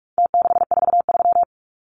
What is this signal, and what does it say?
Morse code, heard as a keyed tone, about 44 wpm, 700 Hz. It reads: T643
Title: Numbers - Contest Style - 4-Digit 45wpm